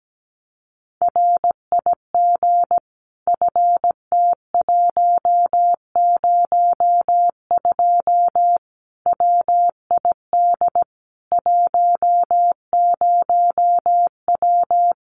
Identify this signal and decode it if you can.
Morse code, heard as a keyed tone, about 17 wpm, 700 Hz. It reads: RIG FT102 WID 10W
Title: Common QSO Phrases 17wpm